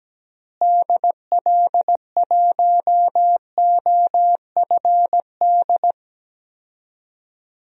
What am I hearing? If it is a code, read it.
Morse code, heard as a keyed tone, about 17 wpm, 700 Hz. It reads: DL1OFD
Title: DXCC-100 Call Signs - 6 Characters - Repeated 3x - 17wpm